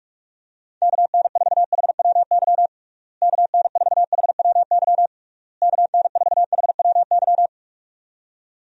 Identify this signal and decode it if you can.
Morse code, heard as a keyed tone, about 42 wpm, 700 Hz. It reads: KN4HWY KN4HWY KN4HWY
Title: USA Call Signs - 6 Characters - Repeated 3x - 45wpm